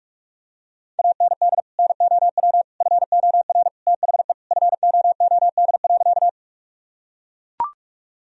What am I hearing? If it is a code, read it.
Morse code, heard as a keyed tone, about 42 wpm, 700 Hz. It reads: AND NOW FOR THE FOOD.
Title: Sentences from Top 1000 Words - Review Part 1 - 45wpm